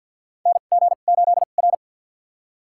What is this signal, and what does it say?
Morse code, heard as a keyed tone, about 47 wpm, spaced as for 35 wpm, 700 Hz. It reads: NG8R
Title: USA Call Signs - 4 Characters - 2x Character Spacing - 50wpm